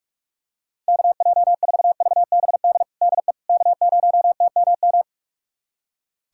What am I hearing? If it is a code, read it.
Morse code, heard as a keyed tone, about 46 wpm, 690 Hz. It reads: KJ4VBD DE K0TMM
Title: USA Callsign DE Callsign 45wpm